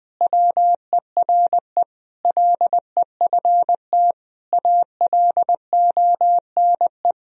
Morse code, heard as a keyed tone, about 20 wpm, 700 Hz. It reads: WERE LEFT ALONE
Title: Sentences from Top 1000 Words - Review Part 1 - 20wpm